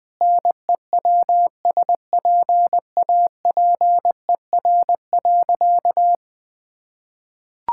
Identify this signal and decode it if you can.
Morse code, heard as a keyed tone, about 20 wpm, 700 Hz. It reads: NEWSPAPER.
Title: Sentences from Top 1000 Words - Review Part 5 - 20wpm